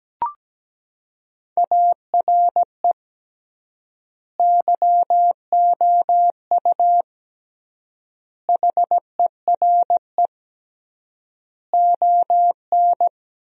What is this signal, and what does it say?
Morse code, heard as a keyed tone, about 17 wpm, 700 Hz. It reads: ARE YOU HERE ON
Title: 3x Word-Spacing - Sentences from Top 400 Words - 17wpm